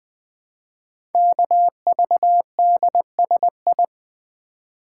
Morse code, heard as a keyed tone, about 20 wpm, 700 Hz. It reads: KVDSI
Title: Code Groups - 5 Letters - 20wpm